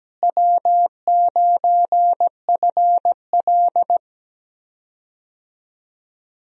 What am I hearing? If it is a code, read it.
Morse code, heard as a keyed tone, about 17 wpm, 700 Hz. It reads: W9FL
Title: USA Call Signs - 4 Characters - Repeated 2x - 17wpm